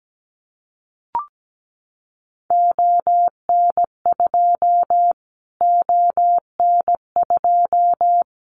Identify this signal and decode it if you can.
Morse code, heard as a keyed tone, about 17 wpm, 700 Hz. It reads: ON2 ON2
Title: Callsign Prefixes for Top 25 DXCC - Repeated 3x - 17wpm